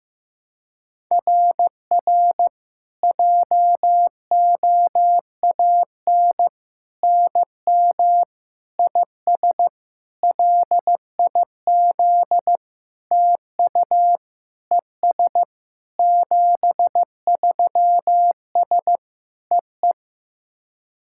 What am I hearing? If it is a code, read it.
Morse code, heard as a keyed tone, about 16 wpm, 700 Hz. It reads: RR JOAN NM IS LIZ TU ES 73S EE
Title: SOTA QSO Practice 15wpm